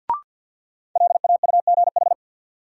Morse code, heard as a keyed tone, about 51 wpm, 700 Hz. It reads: LAUGH
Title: Sentences from Top 500 Words - New Word Drill - 50wpm